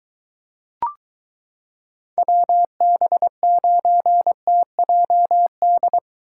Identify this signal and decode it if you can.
Morse code, heard as a keyed tone, about 22 wpm, 700 Hz. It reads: WB9TJD